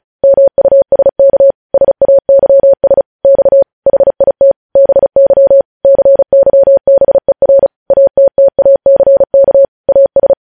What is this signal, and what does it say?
Morse code, heard as a keyed tone, about 36 wpm, 550 Hz. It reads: MUSK SAYS X HIT BY CYBER ATTACK AS
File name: podcast / News headlines-1x-35x35